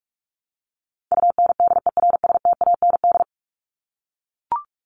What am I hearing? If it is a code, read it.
Morse code, heard as a keyed tone, about 44 wpm, 700 Hz. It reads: UNDERSTAND
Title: Top 501-750 Words 45wpm